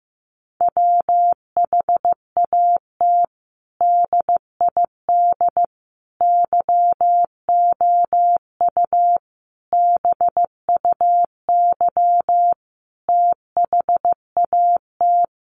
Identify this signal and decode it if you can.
Morse code, heard as a keyed tone, about 15 wpm, 700 Hz. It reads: WHAT DID YOU BUY THAT
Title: Sentences from Top 700 Words 15wpm